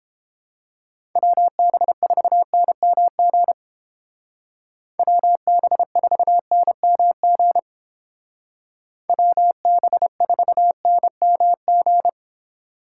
Morse code, heard as a keyed tone, about 29 wpm, 700 Hz. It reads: WB4NMG WB4NMG WB4NMG